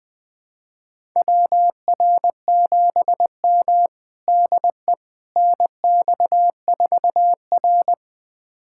Morse code, heard as a keyed tone, about 21 wpm, 700 Hz. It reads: WR7M DE NX4R
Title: USA Callsign DE Callsign - 4 Characters - 20wpm